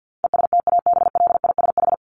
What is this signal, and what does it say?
Morse code, heard as a keyed tone, about 50 wpm, 700 Hz. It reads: ESTABLISH
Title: ED Suffix - Top 100 Words with Base Words 50wpm